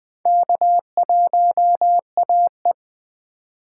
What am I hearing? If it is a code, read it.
Morse code, heard as a keyed tone, about 20 wpm, 700 Hz. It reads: K1AE
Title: K1USN SST QSO Practice - 20wpm